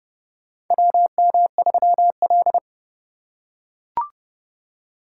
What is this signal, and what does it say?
Morse code, heard as a keyed tone, about 28 wpm, 700 Hz. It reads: WM3L
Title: USA Call Signs - 4 Characters - 3x Character Spacing - 30wpm